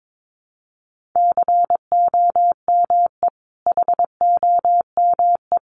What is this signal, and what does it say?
Morse code, heard as a keyed tone, about 22 wpm, 700 Hz. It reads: COME HOME
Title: Sentences from Top 800 Words 22wpm